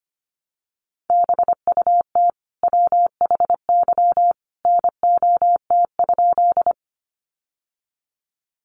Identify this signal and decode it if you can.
Morse code, heard as a keyed tone, about 25 wpm, 700 Hz. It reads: BUT WHY NOT?
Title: Sentences from Top 600 Words 25wpm